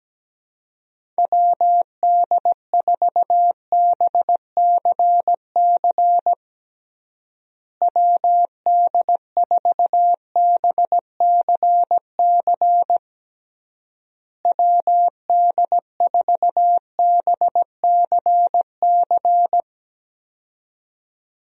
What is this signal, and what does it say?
Morse code, heard as a keyed tone, about 17 wpm, 700 Hz. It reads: WD4BCC WD4BCC WD4BCC
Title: USA Call Signs - 6 Characters - Repeated 3x - 17wpm